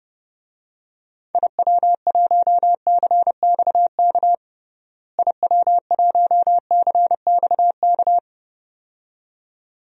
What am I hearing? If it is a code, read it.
Morse code, heard as a keyed tone, about 30 wpm, 700 Hz. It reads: IW1CXK IW1CXK
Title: DXCC-100 Call Signs - 6 Characters - Repeated 2x - 30wpm